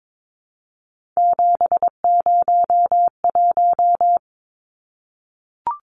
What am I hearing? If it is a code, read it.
Morse code, heard as a keyed tone, about 22 wpm, 700 Hz. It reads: 701
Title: Numbers - 3-Digit 22wpm